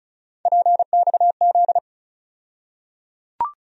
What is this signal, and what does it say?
Morse code, heard as a keyed tone, about 36 wpm, 700 Hz. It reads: PXZ
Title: USA Call Sign Postfixes - 3 Letters 35wpm